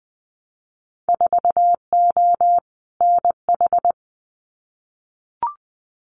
Morse code, heard as a keyed tone, about 20 wpm, 700 Hz. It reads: 4O NH
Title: Winter Field Day Exchanges Encore 20wpm